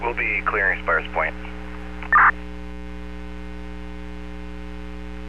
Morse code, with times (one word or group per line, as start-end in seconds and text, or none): none